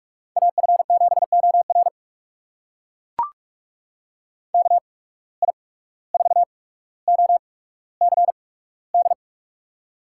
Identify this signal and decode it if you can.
Morse code, heard as a keyed tone, about 42 wpm, spaced as for 18 wpm, 700 Hz. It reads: AF7OR KI4OCD
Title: USA Call Signs - 8x Character Spacing - 45wpm